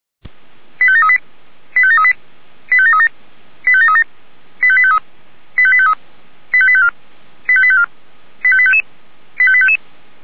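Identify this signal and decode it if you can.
Five-tone selective call (ZVEI), heard as quick runs of five tones, tones of about 70 ms. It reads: 85738 85738 85737 85737 85853 85853 85854 85854 85855 85855
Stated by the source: ZVEI five-tone selective call received on SDR.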